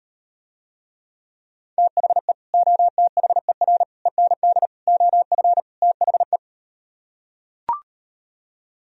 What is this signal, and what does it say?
Morse code, heard as a keyed tone, about 38 wpm, 700 Hz. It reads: THE OTHER END OF THE